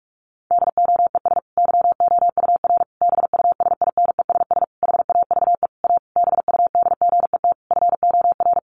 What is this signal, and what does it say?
Morse code, heard as a keyed tone, about 44 wpm, 700 Hz. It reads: DOES YOUR BUSINESS HAVE A BUDGET FOR
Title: Sentences from Top 1200 Words 45wpm